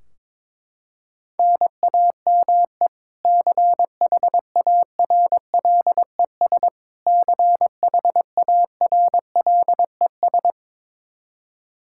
Morse code, heard as a keyed tone, about 22 wpm, 700 Hz. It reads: NAME CHARLES CHARLES
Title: Practice QSO - Exchange 1 - Beginning - 22wpm